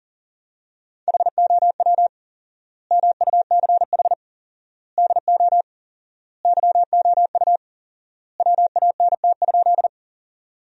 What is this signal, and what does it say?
Morse code, heard as a keyed tone, about 40 wpm, 700 Hz. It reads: HOW MUCH DO YOU WANT?
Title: 4x Word-Spacing - Sentences from Top 200 Words - 40wpm